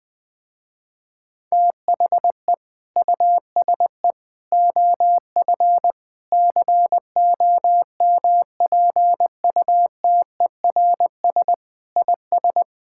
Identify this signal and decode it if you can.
Morse code, heard as a keyed tone, about 20 wpm, 700 Hz. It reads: THE USE OF COMPUTERS IS
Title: Sentences with 3-Word Phrases - Spoken Beginning - 20wpm